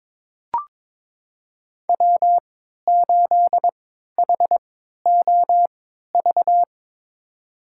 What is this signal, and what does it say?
Morse code, heard as a keyed tone, about 21 wpm, spaced as for 12 wpm, 700 Hz. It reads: W8HOV
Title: USA Call Signs - 3x Character Spacing - 22wpm